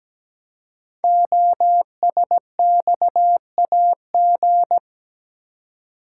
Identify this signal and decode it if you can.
Morse code, heard as a keyed tone, about 17 wpm, 700 Hz. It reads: OSXAG
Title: Code Groups - 5 Letters - 17wpm